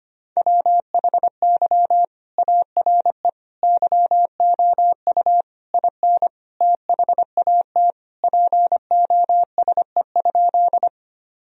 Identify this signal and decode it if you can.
Morse code, heard as a keyed tone, about 25 wpm, 700 Hz. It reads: WHY ARE YOU IN THAT POSE?
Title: Sentences from Top 1000 Words - Review Part 2 - 25wpm